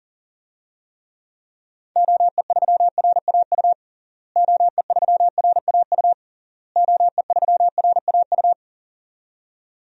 Morse code, heard as a keyed tone, about 40 wpm, 700 Hz. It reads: OE3RAU OE3RAU OE3RAU
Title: DXCC-100 Call Signs - 6 Characters - Repeated 3x - 40wpm